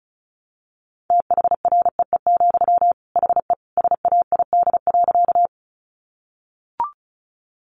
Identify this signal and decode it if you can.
Morse code, heard as a keyed tone, about 34 wpm, 700 Hz. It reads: THREE, HE SAID.
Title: Sentences from Top 200 Words 35wpm